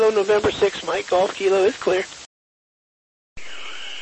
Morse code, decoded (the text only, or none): none